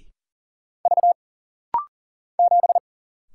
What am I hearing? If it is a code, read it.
Morse code, heard as a keyed tone, about 40 wpm, 700 Hz. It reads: V 7
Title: Single Letter-Number Alt - Rapid-Fire 40wpm